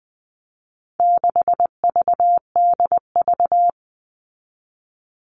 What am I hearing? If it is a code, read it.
Morse code, heard as a keyed tone, about 20 wpm, 700 Hz. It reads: 6VDV
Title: All Cut Numbers - 4-Digit 20wpm